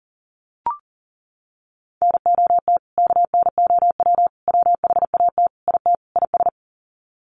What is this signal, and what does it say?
Morse code, heard as a keyed tone, about 37 wpm, 700 Hz. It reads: NOT KNOW WHAT IT IS
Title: Sets of 5 Words Encore 40wpm